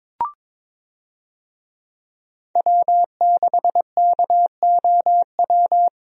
Morse code, heard as a keyed tone, about 22 wpm, 700 Hz. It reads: W6KOW
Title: USA Call Signs - 5 Characters - Repeated 2x - 22wpm